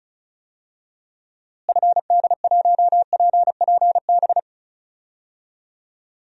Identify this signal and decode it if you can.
Morse code, heard as a keyed tone, about 36 wpm, 700 Hz. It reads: FD1PPB
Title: DXCC-100 Call Signs - 6 Characters - Repeated 3x - 35wpm